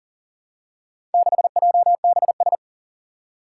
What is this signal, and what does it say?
Morse code, heard as a keyed tone, about 37 wpm, 700 Hz. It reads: 6JBS
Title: Four Challenging Letter-Number Combinations 40wpm